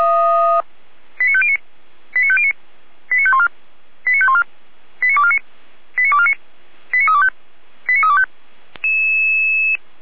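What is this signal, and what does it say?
Five-tone selective call (ZVEI), heard as quick runs of five tones, tones of about 70 ms. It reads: 89509 89509 89514 89514 89249 89249 89325 89325
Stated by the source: ZVEI five-tone selective call received on an SDR, from the feuerwehr-orlamuende fire brigade server.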